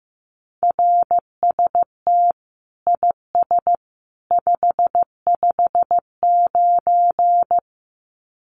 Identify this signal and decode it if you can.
Morse code, heard as a keyed tone, about 15 wpm, 700 Hz. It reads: RST IS 559
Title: Common QSO Phrases 15wpm